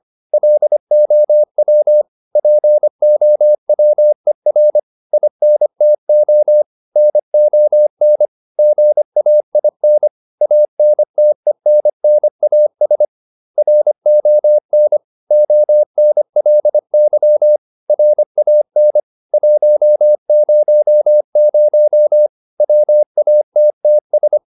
Morse code, heard as a keyed tone, about 25 wpm, 600 Hz. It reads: LOW POWER INTO NON GAIN ANTENNAS RON ONLY RAN 100 WATTS
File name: SS305 25